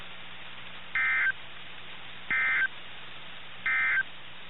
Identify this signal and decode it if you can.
Emergency Alert System (SAME) tones, heard as harsh buzzing bursts. It is heard from 0.8 to 4.1 s.
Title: NOAA Weather Radio - KIH61